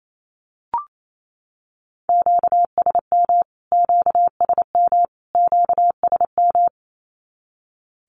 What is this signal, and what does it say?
Morse code, heard as a keyed tone, about 28 wpm, 700 Hz. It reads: QSM QSM QSM
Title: Q-Codes Repeated 3x 28wpm